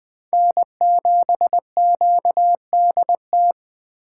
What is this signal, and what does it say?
Morse code, heard as a keyed tone, about 23 wpm, 700 Hz. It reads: N7QDT
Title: Speed Racing - USA 1x3 Call Signs 20wpm